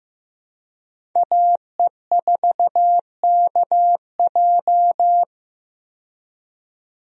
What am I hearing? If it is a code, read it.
Morse code, heard as a keyed tone, about 15 wpm, 700 Hz. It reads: AE4KJ
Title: USA Call Signs - 5 Characters - 15wpm